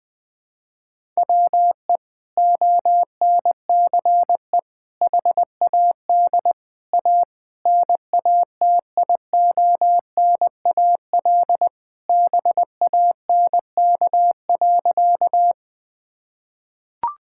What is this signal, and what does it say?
Morse code, heard as a keyed tone, about 20 wpm, 700 Hz. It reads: WE ONCE HAD A NATIONAL BANK.
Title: Sentences from Top 1000 Words - Review Part 6 - 20wpm